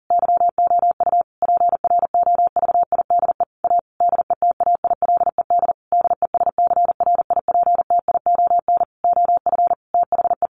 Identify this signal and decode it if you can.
Morse code, heard as a keyed tone, about 37 wpm, 700 Hz. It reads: YOU PROVIDE A DETAILED DESCRIPTION OF THE
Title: Sentences from Top 1500 Words 40wpm